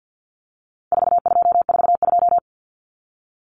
Morse code, heard as a keyed tone, about 50 wpm, 700 Hz. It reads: 4242
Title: Numbers - 4-Digit 50wpm